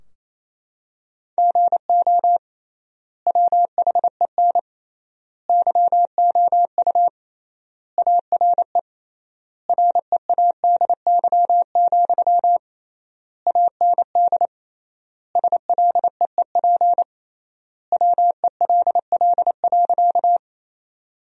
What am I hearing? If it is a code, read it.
Morse code, heard as a keyed tone, about 28 wpm, 700 Hz. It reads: GO WHEN YOU ARE READY, AND SLEEP WELL.
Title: 3x Word-Spacing - Sentences from Top 500 Words - 28wpm